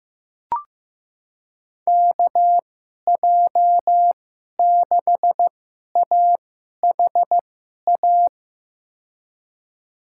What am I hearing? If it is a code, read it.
Morse code, heard as a keyed tone, about 15 wpm, spaced as for 11 wpm, 700 Hz. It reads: KJ6AHA
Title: USA Call Signs - 6 Characters - 2x Character Spacing - 15wpm